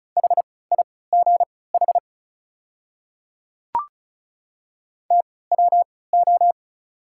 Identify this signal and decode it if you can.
Morse code, heard as a keyed tone, about 34 wpm, spaced as for 20 wpm, 700 Hz. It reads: HIGH TWO
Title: 3x Character Spacing - Top 300 Words - 35wpm